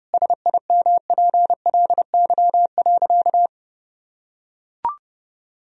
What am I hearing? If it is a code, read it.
Morse code, heard as a keyed tone, about 30 wpm, 700 Hz. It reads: SIMPLY.
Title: Sentences from Top 1000 Words - Review Part 4 - 30wpm